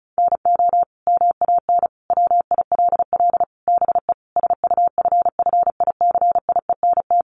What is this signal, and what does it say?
Morse code, heard as a keyed tone, about 34 wpm, 700 Hz. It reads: NO MAN WILL BE SUFFICIENT
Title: Sentences from Top 1400 Words 35wpm